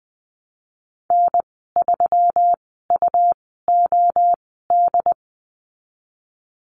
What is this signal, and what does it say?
Morse code, heard as a keyed tone, about 20 wpm, spaced as for 15 wpm, 700 Hz. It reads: N3UOD
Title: 154 Lesson 25g - Letter p - Call Signs 20wpm